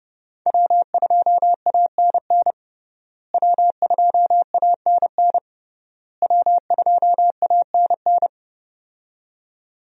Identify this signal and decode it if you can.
Morse code, heard as a keyed tone, about 29 wpm, 700 Hz. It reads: W2ANN W2ANN W2ANN
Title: USA Call Signs - 5 Characters - Repeated 3x - 30wpm